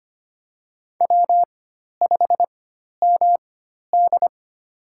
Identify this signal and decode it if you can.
Morse code, heard as a keyed tone, about 25 wpm, spaced as for 12 wpm, 700 Hz. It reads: W5MD
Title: USA Call Signs - 4 Characters - 4x Character Spacing - 25wpm